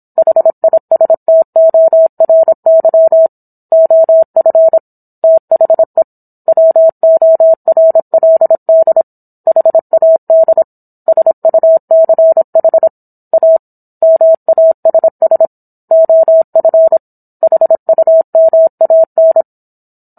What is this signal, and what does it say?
Morse code, heard as a keyed tone, about 27 wpm, 650 Hz. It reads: HISTORY OF THE WORLD HAD SUCH A MASS OF HUMAN
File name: Book One / 17-Chapter 17 - 26 WPM